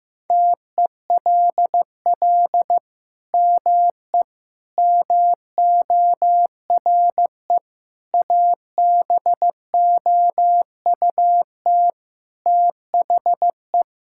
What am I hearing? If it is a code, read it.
Morse code, heard as a keyed tone, about 16 wpm, 700 Hz. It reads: TELL ME MORE ABOUT THE
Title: Sentences from Top 1000 Words - Review Part 5 - 15wpm